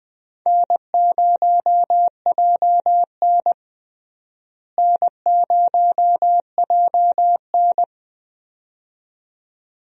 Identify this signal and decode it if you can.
Morse code, heard as a keyed tone, about 21 wpm, 700 Hz. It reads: N0JN N0JN
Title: USA Call Signs - 4 Characters - Repeated 2x - 20wpm